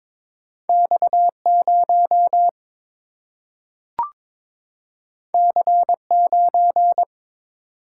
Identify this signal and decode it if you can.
Morse code, heard as a keyed tone, about 22 wpm, 700 Hz. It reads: X 0 C 9
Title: Single-Letter Single-Number 22wpm